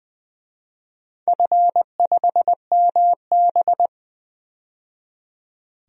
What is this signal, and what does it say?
Morse code, heard as a keyed tone, about 20 wpm, 700 Hz. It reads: F5MB
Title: DXCC-100 Call Signs - 4 Characters - 20wpm